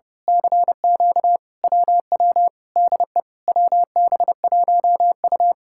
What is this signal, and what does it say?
Morse code, heard as a keyed tone, about 28 wpm, 700 Hz. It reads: CQ WW DE WB1U
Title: Speed-Racing Common QSO Phrases - 30wpm